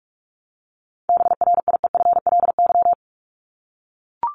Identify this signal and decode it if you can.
Morse code, heard as a keyed tone, about 44 wpm, 700 Hz. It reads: BRIEFLY
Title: LY Suffix - Top 100 Words with Base Words 45wpm